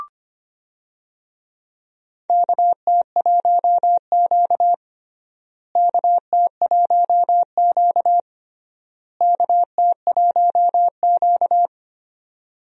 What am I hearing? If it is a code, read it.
Morse code, heard as a keyed tone, about 25 wpm, 700 Hz. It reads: KT1Q KT1Q KT1Q
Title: USA Call Signs - 4 Characters - Repeated 3x - 25wpm